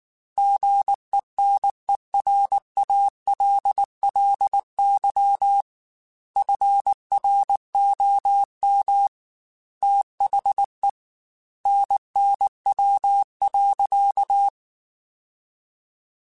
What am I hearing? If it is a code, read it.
Morse code, heard as a keyed tone, about 19 wpm, 800 Hz. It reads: GENERALLY FROM THE NNW.